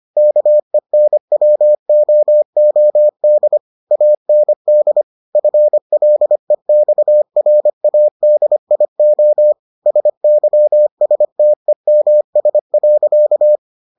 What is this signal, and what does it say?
Morse code, heard as a keyed tone, about 25 wpm, 600 Hz. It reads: KENWOOD AND FLEXRADIO SYSTEMS.